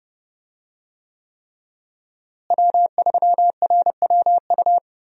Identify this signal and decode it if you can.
Morse code, heard as a keyed tone, about 32 wpm, 700 Hz. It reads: W3RWU
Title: USA Call Signs - 5 Characters - Repeated 3x - 30wpm